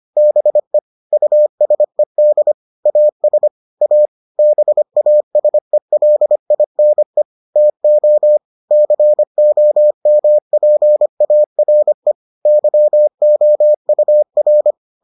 Morse code, heard as a keyed tone, about 25 wpm, 600 Hz. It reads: BE USED AS A BASELINE TO COMPARE YOUR